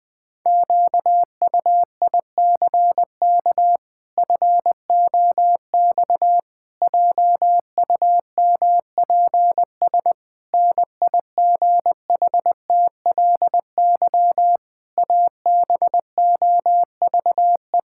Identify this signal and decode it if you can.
Morse code, heard as a keyed tone, about 20 wpm, 700 Hz. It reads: QUICK FOX JUMPS NIGHTLY ABOVE
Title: Pangram Challenge v2 - 20wpm